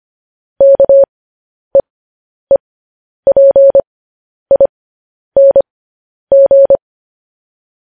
Morse code, heard as a keyed tone, about 25 wpm, spaced as for 10 wpm, 550 Hz. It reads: KEEPING
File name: podcast / News headlines-1x-25x10